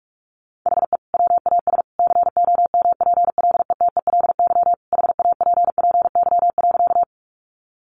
Morse code, heard as a keyed tone, about 44 wpm, 700 Hz. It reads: HE WAS COMPLETELY HAPPY.
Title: Sentences from Top 1100 Words 45wpm